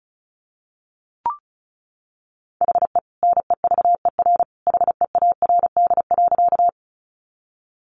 Morse code, heard as a keyed tone, about 34 wpm, 700 Hz. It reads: HE NEVER HEARD.
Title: Sentences from Top 400 Words 35wpm